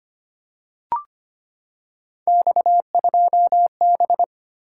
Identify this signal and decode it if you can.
Morse code, heard as a keyed tone, about 25 wpm, 700 Hz. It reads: X2B
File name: Three Letter-Number Combinations - 25wpm